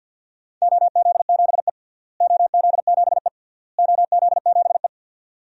Morse code, heard as a keyed tone, about 50 wpm, 700 Hz. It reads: OZ7E OZ7E OZ7E
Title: DXCC-100 Call Signs - 4 Characters - Repeated 3x - 50wpm